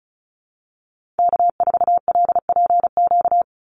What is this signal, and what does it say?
Morse code, heard as a keyed tone, about 36 wpm, 700 Hz. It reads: K4LPQ
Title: NA SOTA Activator Callsigns 35wpm